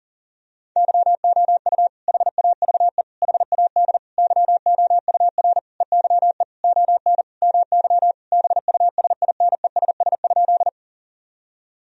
Morse code, heard as a keyed tone, about 40 wpm, 700 Hz. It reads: YOU HAVE HAD YOUR EYE ON MY BUSINESS?
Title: Sentences from Top 1000 Words - Review Part 1 - 40wpm